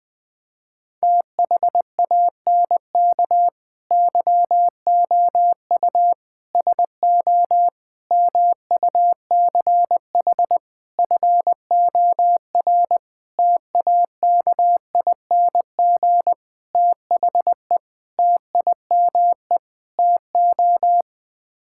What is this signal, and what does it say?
Morse code, heard as a keyed tone, about 20 wpm, 700 Hz. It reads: THANK YOU SO MUCH FOR TAKING THE TIME TO
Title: Sets of 9 Words - 20wpm